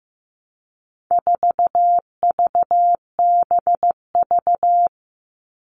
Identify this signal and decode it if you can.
Morse code, heard as a keyed tone, about 15 wpm, 700 Hz. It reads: 4VBV
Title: All Cut Numbers - 4-Digit 15wpm